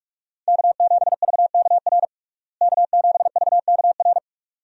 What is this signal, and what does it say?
Morse code, heard as a keyed tone, about 41 wpm, 700 Hz. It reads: K7VKR K7VKR
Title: USA Call Signs - 5 Characters - Repeated 2x - 45wpm